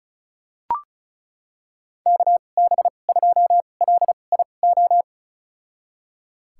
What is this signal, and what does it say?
Morse code, heard as a keyed tone, about 34 wpm, spaced as for 25 wpm, 700 Hz. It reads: KB2LIO
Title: USA Call Signs - 6 Characters - 2x Character Spacing - 35wpm